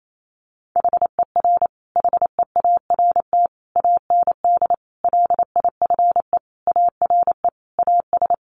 Morse code, heard as a keyed tone, about 28 wpm, 700 Hz. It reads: HER HEART AND LIFE ARE AS
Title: Sentences from Top 1500 Words 28wpm